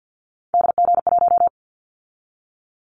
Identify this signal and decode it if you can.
Morse code, heard as a keyed tone, about 50 wpm, 700 Hz. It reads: DG1
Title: Callsign Prefixes for Top 25 DXCC - 50wpm